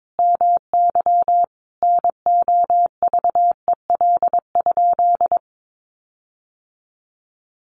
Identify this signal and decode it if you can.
Morse code, heard as a keyed tone, about 22 wpm, 700 Hz. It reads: MY NOVEL?
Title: Sentences from Top 1300 Words 22wpm